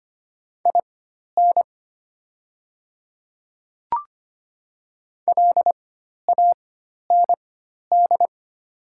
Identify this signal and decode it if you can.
Morse code, heard as a keyed tone, about 25 wpm, spaced as for 12 wpm, 700 Hz. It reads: IN LAND
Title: 4x Character Spacing - Top 200 Words - 25wpm